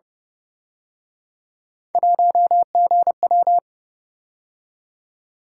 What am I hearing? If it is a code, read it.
Morse code, heard as a keyed tone, about 30 wpm, 700 Hz. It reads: 1GW
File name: Speed-Racing - Three Letter-Number Combinations - 25wpm